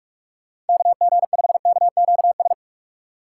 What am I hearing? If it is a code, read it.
Morse code, heard as a keyed tone, about 44 wpm, 700 Hz. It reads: KG5KQS
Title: USA Call Signs - 6 Characters - 4x Character Spacing - 45wpm